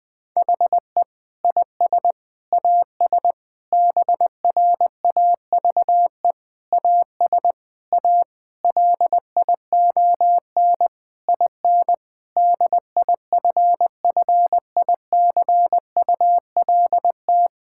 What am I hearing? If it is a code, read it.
Morse code, heard as a keyed tone, about 21 wpm, 700 Hz. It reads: HE IS AS BRAVE AS A LION IN DIFFICULT
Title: Difficult Sentences with Hints - 20wpm